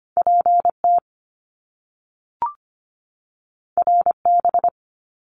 Morse code, heard as a keyed tone, about 25 wpm, 700 Hz. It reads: PT RB
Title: Top Two Letter Combinations in Words 25wpm